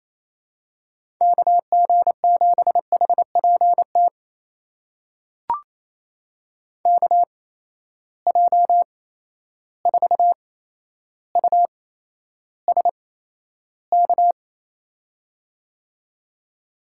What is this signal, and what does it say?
Morse code, heard as a keyed tone, about 28 wpm, spaced as for 12 wpm, 700 Hz. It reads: KG7HPT KJ4USK
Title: USA Call Signs - 6 Characters - 8x Character Spacing - 28wpm